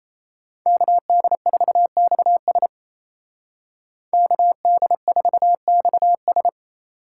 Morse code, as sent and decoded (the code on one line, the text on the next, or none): -.- -.. ....- -..- ... / -.- -.. ....- -..- ...
KD4XS KD4XS